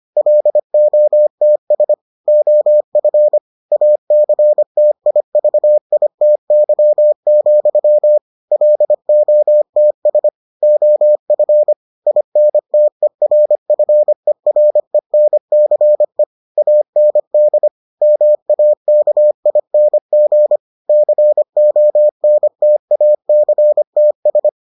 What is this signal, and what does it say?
Morse code, heard as a keyed tone, about 25 wpm, 600 Hz. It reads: LOTS OF ACTIVITY, LOTS OF INTERFERENCE AND MAKING CONTACTS